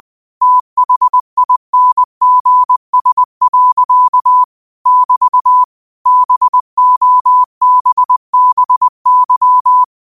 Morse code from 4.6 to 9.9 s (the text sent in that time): <BT> BOBBY